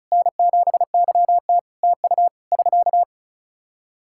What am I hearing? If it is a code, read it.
Morse code, heard as a keyed tone, about 34 wpm, 700 Hz. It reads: N7YT TU <SK>
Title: Practice QSO - Exchange 4 - Ending - 35wpm